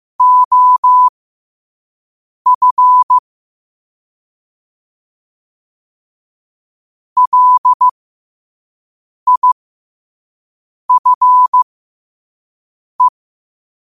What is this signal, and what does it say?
Morse code, heard as a keyed tone, about 15 wpm, spaced as for 5 wpm, 1000 Hz. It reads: OF LIFE
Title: 5 WPM morse code quotes for Fri, 01 Mar 2024